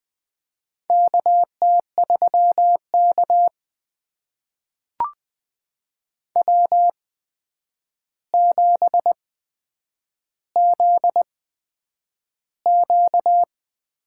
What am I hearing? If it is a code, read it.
Morse code, heard as a keyed tone, about 20 wpm, spaced as for 9 wpm, 700 Hz. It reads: KT3K W7ZQ
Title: USA Call Signs - 4 Characters - 8x Character Spacing - 20wpm